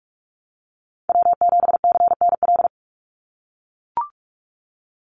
Morse code, heard as a keyed tone, about 44 wpm, 700 Hz. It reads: W7CNL
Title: NA SOTA Chaser Callsigns 45wpm